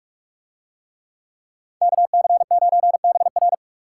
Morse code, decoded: KC9BR

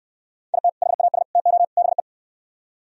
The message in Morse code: IT HAS TO BE

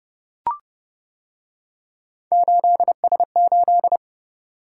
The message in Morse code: 8S8

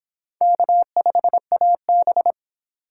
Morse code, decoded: K5AB

